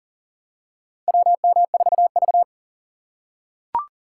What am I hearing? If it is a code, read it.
Morse code, heard as a keyed tone, about 40 wpm, 700 Hz. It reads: WM4V
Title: USA Call Signs - 4 Characters - 4x Character Spacing - 40wpm